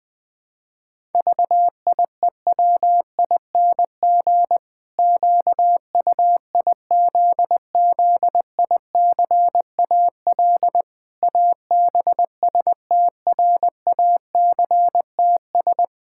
Morse code, heard as a keyed tone, about 21 wpm, 700 Hz. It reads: VIEWING QUIZZICAL ABSTRACTS
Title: Pangram Challenge v2 - 20wpm